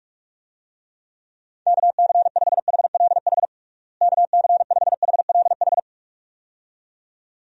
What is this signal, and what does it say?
Morse code, heard as a keyed tone, about 41 wpm, 700 Hz. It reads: KC5HLH KC5HLH
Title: USA Call Signs - 6 Characters - Repeated 2x - 45wpm